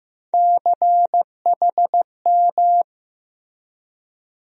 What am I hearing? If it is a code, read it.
Morse code, heard as a keyed tone, about 15 wpm, 700 Hz. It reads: CHM
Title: USA Call Sign Postfixes - 3 Letters 15wpm